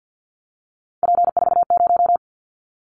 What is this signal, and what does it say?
Morse code, heard as a keyed tone, about 50 wpm, 700 Hz. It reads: P40
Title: Callsign Prefixes for Top 100 DXCC - 50wpm